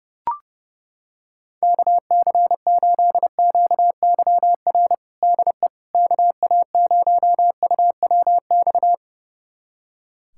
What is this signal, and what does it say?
Morse code, heard as a keyed tone, about 32 wpm, 700 Hz. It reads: KC8QYR DE KA0UWX
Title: USA Callsign DE Callsign - 6 Characters - 30wpm